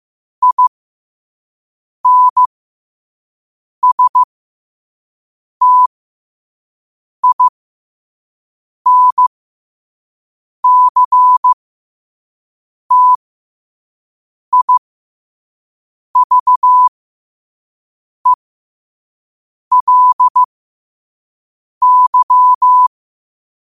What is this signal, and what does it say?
Morse code, heard as a keyed tone, about 15 wpm, spaced as for 5 wpm, 1000 Hz. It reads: INSTINCTIVELY
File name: qotdfiles / 5wpm